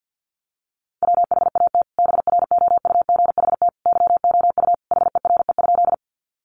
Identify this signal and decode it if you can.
Morse code, heard as a keyed tone, about 50 wpm, 700 Hz. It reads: WHAT BROUGHT YOU HERE?
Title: Sentences from Top 500 Words 50wpm